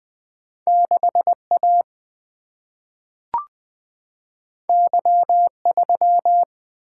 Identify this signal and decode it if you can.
Morse code, heard as a keyed tone, about 20 wpm, 700 Hz. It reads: 6A Y3
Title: Single-Letter Single-Number 20wpm